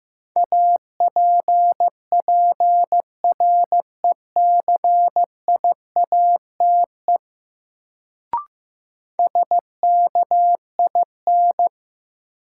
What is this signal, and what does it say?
Morse code, heard as a keyed tone, about 15 wpm, 700 Hz. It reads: APPRECIATE SKIN
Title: Sentences from Top 1300 Words - New Word Drill - 15wpm